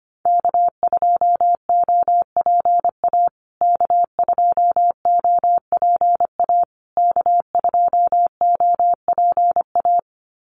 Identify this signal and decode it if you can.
Morse code, heard as a keyed tone, about 25 wpm, 700 Hz. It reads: K2OPA K2OPA K2OPA
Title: USA Call Signs Repeated 3x 25wpm